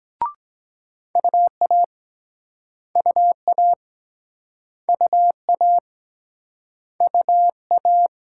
Morse code, heard as a keyed tone, about 23 wpm, 700 Hz. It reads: UA UA UA UA